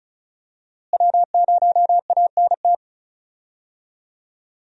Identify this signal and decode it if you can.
Morse code, heard as a keyed tone, about 34 wpm, 700 Hz. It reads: W0ANT
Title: USA Call Signs - 5 Characters - Repeated 2x - 35wpm